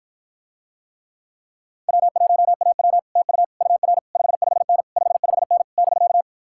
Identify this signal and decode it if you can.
Morse code, heard as a keyed tone, about 54 wpm, 700 Hz. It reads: W1AW TU UR 55N 55N <BK>